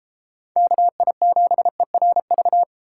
Morse code, heard as a keyed tone, about 33 wpm, 700 Hz. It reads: KI7ERV